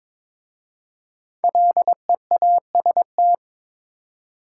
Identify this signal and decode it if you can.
Morse code, heard as a keyed tone, about 22 wpm, 700 Hz. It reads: LEAST
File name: 5 Letter Words - First 3 Letters Given - 22wpm